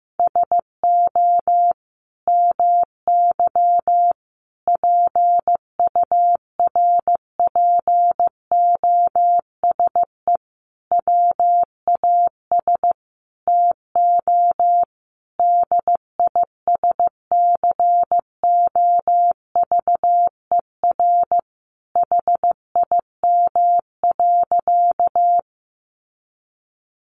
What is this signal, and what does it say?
Morse code, heard as a keyed tone, about 15 wpm, 700 Hz. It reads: SO MY PURPOSE WAS TO DISCOVER HIM.
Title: Sentences from Top 1100 Words 15wpm